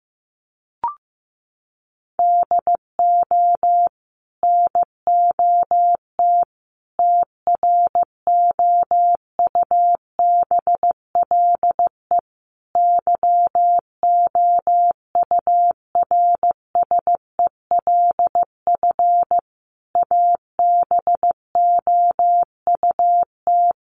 Morse code, heard as a keyed tone, about 15 wpm, 700 Hz. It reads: DO NOT TROUBLE YOURSELF ABOUT
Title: Sentences from Top 1200 Words 15wpm